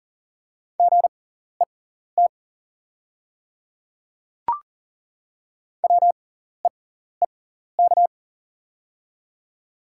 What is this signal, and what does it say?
Morse code, heard as a keyed tone, about 40 wpm, spaced as for 14 wpm, 690 Hz. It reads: GET WEEK
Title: 6x Character Spacing - Top 500 Words - 40wpm